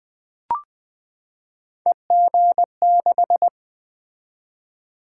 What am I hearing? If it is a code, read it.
Morse code, heard as a keyed tone, about 20 wpm, 700 Hz. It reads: EG6
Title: Three Letter-Number Combinations - 20wpm